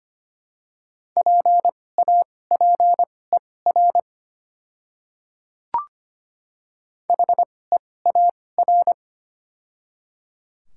Morse code, heard as a keyed tone, about 25 wpm, spaced as for 18 wpm, 690 Hz. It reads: PAPER HEAR
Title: 2x Character Spacing - Top 300 Words - 25wpm